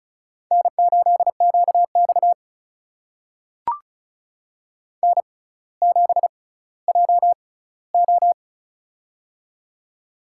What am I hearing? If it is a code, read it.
Morse code, heard as a keyed tone, about 36 wpm, spaced as for 18 wpm, 700 Hz. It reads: N8QX N7JO
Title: USA Call Signs - 4 Characters - 6x Character Spacing - 35wpm